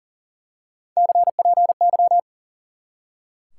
Morse code, heard as a keyed tone, about 37 wpm, 700 Hz. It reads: CPY
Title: Top 180 QSO Elements 40wpm